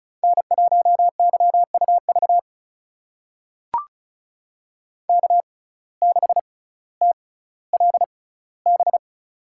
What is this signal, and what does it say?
Morse code, heard as a keyed tone, about 36 wpm, spaced as for 18 wpm, 700 Hz. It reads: N1YUV K6TLB
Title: USA Call Signs - 5 Characters - 6x Character Spacing - 35wpm